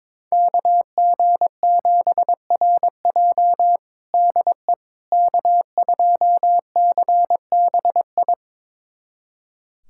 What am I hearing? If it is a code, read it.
Morse code, heard as a keyed tone, about 22 wpm, 700 Hz. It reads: KG7RJ DE K2CBI
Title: USA Callsign DE Callsign - 5 Characters - 22wpm